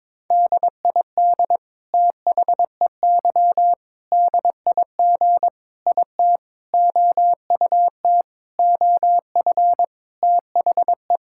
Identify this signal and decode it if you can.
Morse code, heard as a keyed tone, about 22 wpm, 700 Hz. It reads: DID THEY DIG IT OUT OF THE
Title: Sentences from Top 1600 Words 22wpm